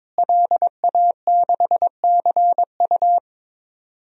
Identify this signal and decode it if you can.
Morse code, heard as a keyed tone, about 22 wpm, 700 Hz. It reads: LA6CU
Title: DXCC-100 Call Signs - 5 Characters - Repeated 3x - 22wpm